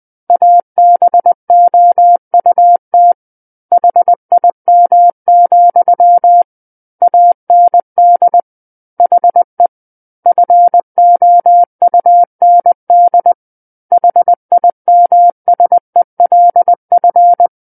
Morse code, heard as a keyed tone, about 20 wpm, 700 Hz. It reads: ABOUT HIM, AND HE FOUND HIMSELF
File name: 20 WPM